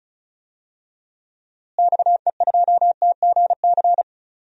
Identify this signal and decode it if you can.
Morse code, heard as a keyed tone, about 36 wpm, 700 Hz. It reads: XE2TGC
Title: DXCC-100 Call Signs - 6 Characters - Repeated 3x - 35wpm